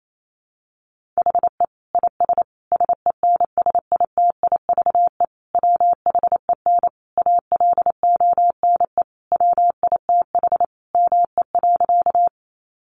Morse code, heard as a keyed tone, about 28 wpm, 700 Hz. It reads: HE IS SENSITIVE WHEN ALONE WITH ME.
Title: Sentences from Top 1400 Words 28wpm